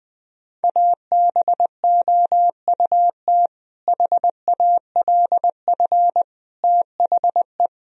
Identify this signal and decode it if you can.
Morse code, heard as a keyed tone, about 20 wpm, 700 Hz. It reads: ABOUT HALF THE
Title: Sentences from Top 1000 Words - Review Part 1 - 20wpm